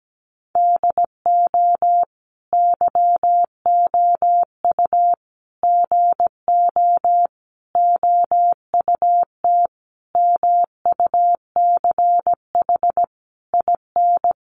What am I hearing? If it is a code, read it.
Morse code, heard as a keyed tone, about 17 wpm, 700 Hz. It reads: DO YOU GO OUT MUCH IN
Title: Sentences from Top 700 Words 17wpm